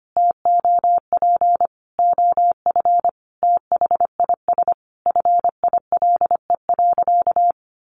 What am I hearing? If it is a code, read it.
Morse code, heard as a keyed tone, about 25 wpm, 700 Hz. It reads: TOP OF THIS FILE.
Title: Sentences from Top 1200 Words 25wpm